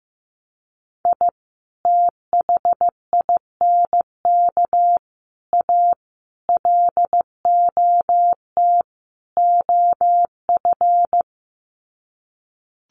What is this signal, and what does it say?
Morse code, heard as a keyed tone, about 15 wpm, 700 Hz. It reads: I THINK A LOT OF
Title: Sets of 5 Words 15wpm